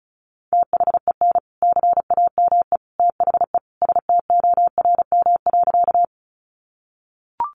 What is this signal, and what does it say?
Morse code, heard as a keyed tone, about 34 wpm, 700 Hz. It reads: THEN CAME THE STORM.
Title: Sentences from Top 1500 Words 35wpm